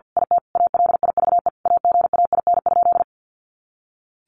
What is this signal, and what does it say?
Morse code, heard as a keyed tone, about 48 wpm, 700 Hz. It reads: IT ALIVE AGAIN?
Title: A Sound Of Thunder 50wpm